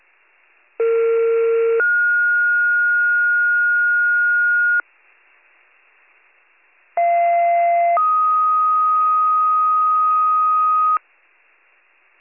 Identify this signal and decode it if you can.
Two-tone page, heard as a one-second tone, then a three-second tone, about 460 then 1430 Hz, about 690 then 1250 Hz. It is heard from 0.8 to 11.0 s.